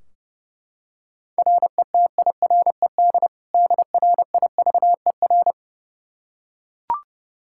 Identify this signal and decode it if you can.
Morse code, heard as a keyed tone, about 30 wpm, 700 Hz. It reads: RETIRED DRIVER
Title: K1USN SST QSO Practice 30wpm